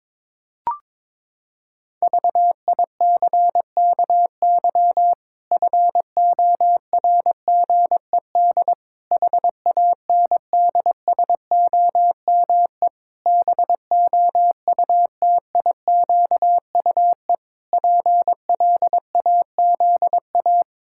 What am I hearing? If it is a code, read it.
Morse code, heard as a keyed tone, about 21 wpm, 700 Hz. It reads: VICKY FORGED HANDSOME BOUTIQUE PLAZA
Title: Pangram Challenge v2 - 22wpm